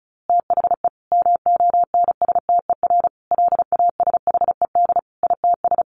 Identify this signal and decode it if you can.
Morse code, heard as a keyed tone, about 34 wpm, 700 Hz. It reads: THE MONSTER LASHED ITS
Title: A Sound Of Thunder 35wpm